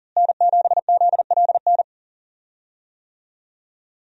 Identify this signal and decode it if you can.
Morse code, heard as a keyed tone, about 37 wpm, 700 Hz. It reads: N7ZLN
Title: USA Call Signs - 5 Characters - Repeated 3x - 40wpm